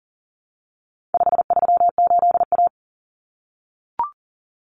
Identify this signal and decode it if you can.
Morse code, heard as a keyed tone, about 37 wpm, 700 Hz. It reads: 538A
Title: Common Cut Numbers - 4-Digit 40wpm